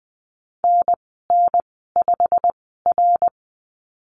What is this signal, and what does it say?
Morse code, heard as a keyed tone, about 20 wpm, spaced as for 15 wpm, 700 Hz. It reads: NN5R
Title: 088 Lesson 15g - Number 5 - Call Signs 20wpm